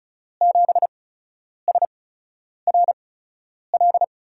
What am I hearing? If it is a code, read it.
Morse code, heard as a keyed tone, about 34 wpm, spaced as for 10 wpm, 700 Hz. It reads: 7SRL
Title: USA Call Signs - 5 Characters - 8x Character Spacing - 35wpm